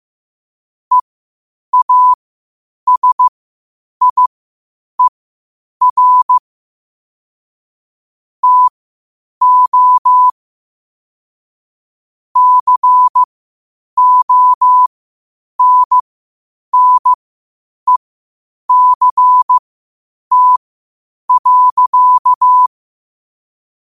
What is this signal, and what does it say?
Morse code, heard as a keyed tone, about 15 wpm, spaced as for 8 wpm, 1000 Hz. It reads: EASIER TO CONNECT.
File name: qotdfiles / 8wpm